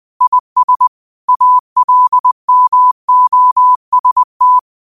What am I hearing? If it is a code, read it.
Morse code, heard as a keyed tone, about 20 wpm, 1000 Hz. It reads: IS ALMOST
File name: qotdfiles / 20wpm